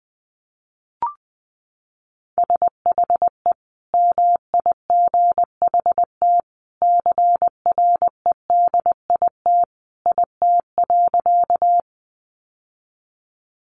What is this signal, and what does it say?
Morse code, heard as a keyed tone, about 20 wpm, 700 Hz. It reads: SHE MIGHT CREDIT IT.
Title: Sentences from Top 1000 Words 20wpm